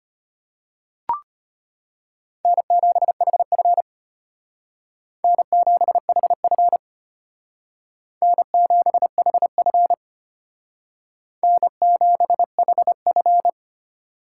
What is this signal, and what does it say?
Morse code, heard as a keyed tone, about 32 wpm, 700 Hz. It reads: N7HF N7HF N7HF N7HF